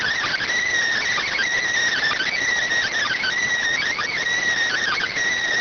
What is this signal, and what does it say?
Emergency Alert System (SAME) tones, heard as harsh buzzing bursts. It is heard from 1.1 to 5.1 s.